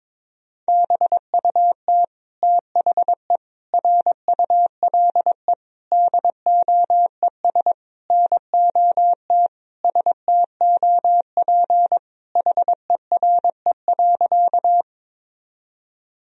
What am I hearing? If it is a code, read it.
Morse code, heard as a keyed tone, about 22 wpm, 700 Hz. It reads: BUT THE RULE DOES NOT STOP HERE.
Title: Sentences from Top 1000 Words - Review Part 2 - 22wpm